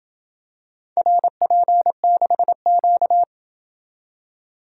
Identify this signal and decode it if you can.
Morse code, heard as a keyed tone, about 28 wpm, 700 Hz. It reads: RP6Q